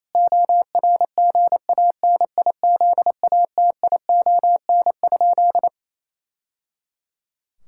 Morse code, heard as a keyed tone, about 27 wpm, 700 Hz. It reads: ORGANIZATION?
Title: Sentences from Top 1000 Words - Review Part 5 - 28wpm